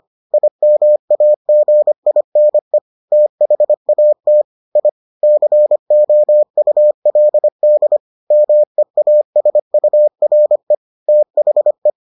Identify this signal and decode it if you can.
Morse code, heard as a keyed tone, about 25 wpm, 600 Hz. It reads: IMAGINE THAT I COULD MEASURE THE